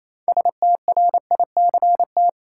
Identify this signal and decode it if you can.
Morse code, heard as a keyed tone, about 28 wpm, 700 Hz. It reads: STRICT
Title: Six Letter Words - 28wpm